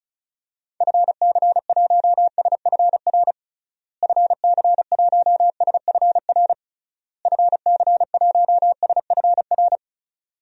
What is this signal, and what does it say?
Morse code, heard as a keyed tone, about 36 wpm, 700 Hz. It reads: FC1SFR FC1SFR FC1SFR
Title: DXCC-100 Call Signs - 6 Characters - Repeated 3x - 35wpm